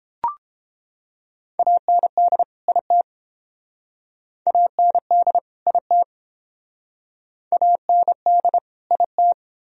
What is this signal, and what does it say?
Morse code, heard as a keyed tone, about 30 wpm, 700 Hz. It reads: AND IT AND IT AND IT